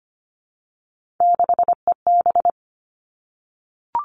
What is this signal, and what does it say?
Morse code, heard as a keyed tone, about 25 wpm, 700 Hz. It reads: 6EB
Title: Three Challenging Letter-Number Combinations 25wpm